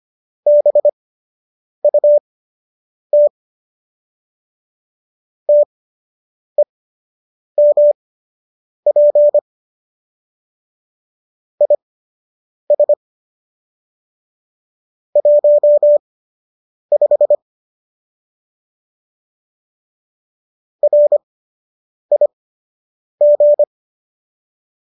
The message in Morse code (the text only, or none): BUT TEMP IS 15 RIG